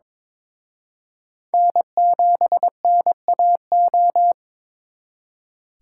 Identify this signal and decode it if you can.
Morse code, heard as a keyed tone, about 26 wpm, 700 Hz. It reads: N7NAO
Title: Speed Racing - USA 1x3 Call Signs 22wpm